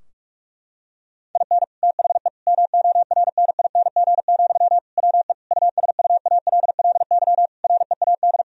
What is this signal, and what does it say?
Morse code, heard as a keyed tone, about 46 wpm, 700 Hz. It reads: IN THE MORNING, WE USUALLY READ
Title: Sentences from Top 1000 Words - Review Part 3 - 45wpm